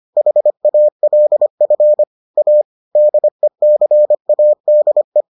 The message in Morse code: HALF A DECADE

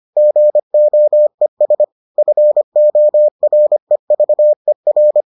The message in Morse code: GOES FOREVER

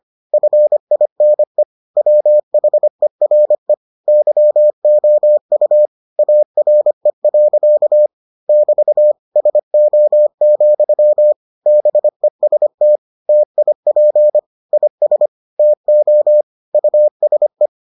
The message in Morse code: FINE WHERE YOU ARE. <BT> SO, BEST TIP IS TO USE

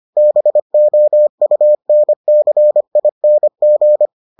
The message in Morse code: BOUNCING

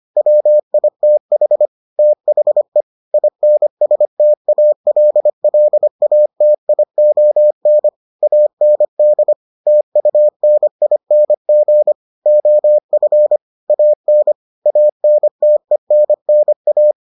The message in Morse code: WITH THE INSTALLATION AND TUNING OF AN ANTENNA

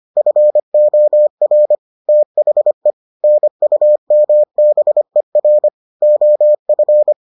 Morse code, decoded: FOR THE NUMBER OF

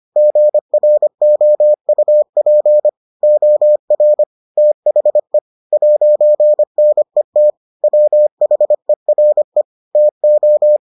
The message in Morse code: GROUP OR THE 'NET WHERE TO

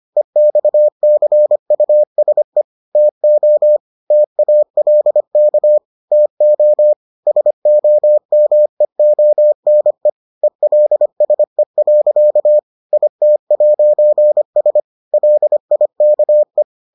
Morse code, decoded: EXCUSE TO TALK TO SOMEONE ELSE. IT'S LIKE

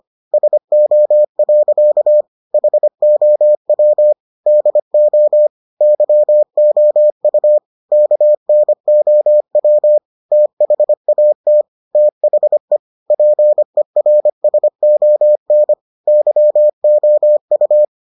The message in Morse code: SO. HOW DO YOU KNOW THAT THE PERSON YOU

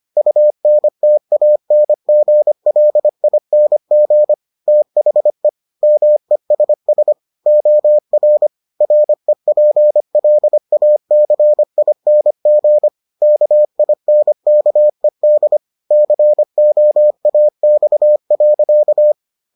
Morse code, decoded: UNTANGLING THE MESS OR REPLACING KINKED COAX.

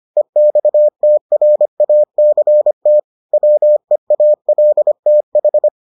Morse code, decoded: EXTRACT WEALTH

